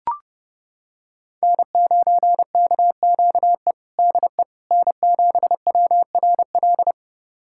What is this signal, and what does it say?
Morse code, heard as a keyed tone, about 30 wpm, 700 Hz. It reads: N9KQE DE N7WRL
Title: USA Callsign DE Callsign - 5 Characters - 30wpm